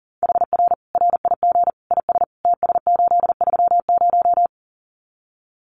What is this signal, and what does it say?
Morse code, heard as a keyed tone, about 40 wpm, 700 Hz. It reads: HR RIG IS TS830
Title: Common QSO Phrases 40wpm